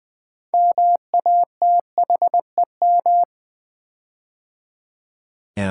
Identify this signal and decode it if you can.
Morse code, heard as a keyed tone, about 20 wpm, 700 Hz. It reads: MATHEM
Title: Build A Word - 6 Letters At A Time - 12-Letter Words - 20wpm